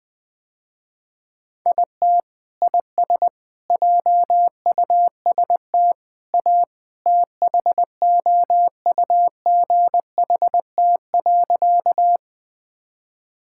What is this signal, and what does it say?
Morse code, heard as a keyed tone, about 20 wpm, 700 Hz. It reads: IT IS JUST A THOUGHT.
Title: Sentences with 2-Word Phrases - Spoken Beginning - 20wpm